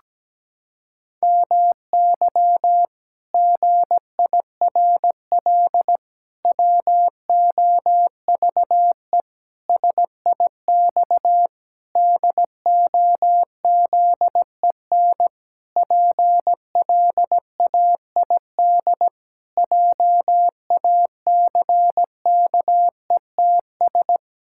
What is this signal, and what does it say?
Morse code, heard as a keyed tone, about 17 wpm, 700 Hz. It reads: MY GIRL WOVE SIX DOZEN PLAID JACKETS
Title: Pangram Challenge v2 - 17wpm